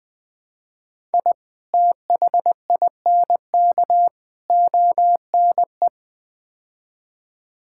Morse code, heard as a keyed tone, about 20 wpm, 700 Hz. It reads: I THINK ONE
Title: Build A Phrase - 5-Word Phrases - 20wpm